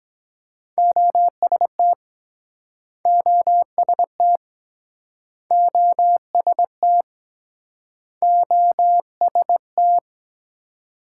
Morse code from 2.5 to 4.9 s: OST

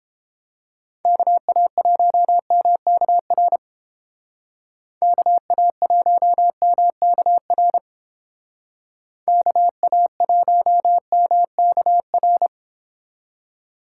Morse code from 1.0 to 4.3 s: KA1MKR

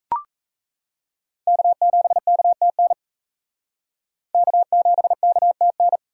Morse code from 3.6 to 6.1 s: K7KTN

